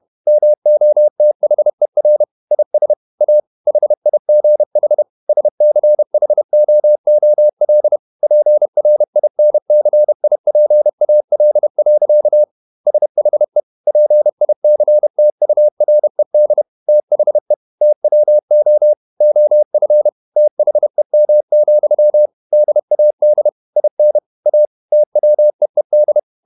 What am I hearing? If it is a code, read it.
Morse code, heard as a keyed tone, about 31 wpm, 600 Hz. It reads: MOTHER IS A HIGH SCHOOL PRINCIPAL. SHE PICTURED THE TWO OF THEM, DAD IN A TWEED
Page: Never 31wpm | CW med Gnister